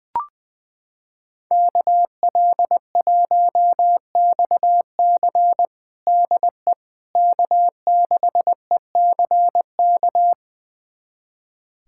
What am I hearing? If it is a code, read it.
Morse code, heard as a keyed tone, about 20 wpm, 700 Hz. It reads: KL1XC DE K6ECK
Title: USA Callsign DE Callsign - 5 Characters - 20wpm